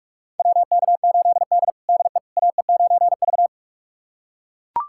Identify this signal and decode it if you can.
Morse code, heard as a keyed tone, about 41 wpm, 700 Hz. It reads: WK8D DE AE9V
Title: USA Callsign DE Callsign - 4 Characters - 45wpm